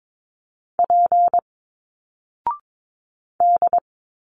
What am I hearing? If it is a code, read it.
Morse code, heard as a keyed tone, about 22 wpm, 700 Hz. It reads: P D
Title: Single Letters 22wpm